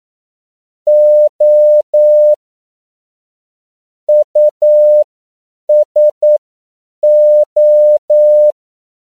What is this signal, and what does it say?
Morse code, heard as a keyed tone, about 9 wpm, spaced as for 7 wpm, 600 Hz. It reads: O USO
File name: CW / 10PPM